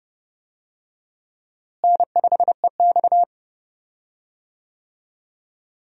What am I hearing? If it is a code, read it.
Morse code, heard as a keyed tone, about 30 wpm, 700 Hz. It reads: N5EX
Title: USA Call Signs - 4 Characters - Repeated 2x - 30wpm